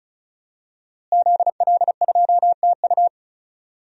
Morse code, heard as a keyed tone, about 34 wpm, 700 Hz. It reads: ZL2TU
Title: DXCC-100 Call Signs - 5 Characters - Repeated 3x - 35wpm